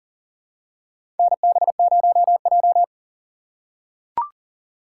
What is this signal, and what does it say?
Morse code, heard as a keyed tone, about 40 wpm, 700 Hz. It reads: NB0J
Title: USA Call Signs - 2x Character Spacing - 40wpm